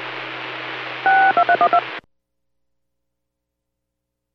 DTMF, heard as telephone touch tones, tones of about 80 ms. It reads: 62312